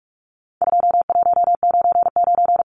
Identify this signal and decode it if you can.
Morse code, heard as a keyed tone, about 44 wpm, 700 Hz. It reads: 2199
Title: Numbers - 4-Digit 45wpm